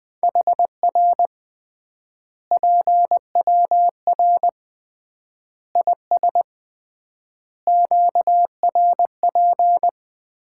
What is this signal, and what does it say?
Morse code, heard as a keyed tone, about 20 wpm, 700 Hz. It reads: HR PWR IS QRP
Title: Common QSO Phrases - 3x Word-Spacing - 20wpm